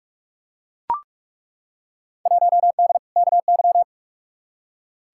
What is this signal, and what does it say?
Morse code, heard as a keyed tone, about 42 wpm, 700 Hz. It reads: 1D KY
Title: Field Day Exchanges - 45wpm